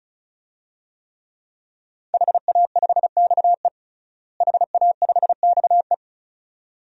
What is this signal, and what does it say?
Morse code, heard as a keyed tone, about 36 wpm, 700 Hz. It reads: HA5XE HA5XE
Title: DXCC-100 Call Signs - 5 Characters - Repeated 2x - 35wpm